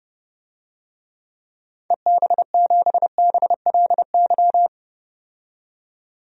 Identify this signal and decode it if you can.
Morse code, heard as a keyed tone, about 30 wpm, 700 Hz. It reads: EB7BLY
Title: DXCC-100 Call Signs - 6 Characters - 30wpm